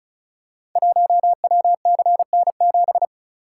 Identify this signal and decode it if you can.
Morse code, heard as a keyed tone, about 36 wpm, 690 Hz. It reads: 1WCN7
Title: Code Groups - 5 Letter:Numbers - 35wpm